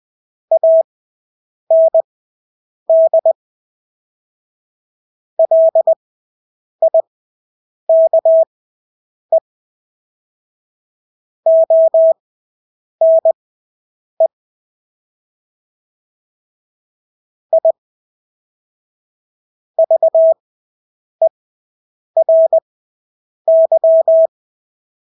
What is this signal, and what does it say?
Morse code, heard as a keyed tone, about 20 wpm, spaced as for 8 wpm, 650 Hz. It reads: AND LIKE ONE I VERY
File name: Engelske ord 20/8 wpm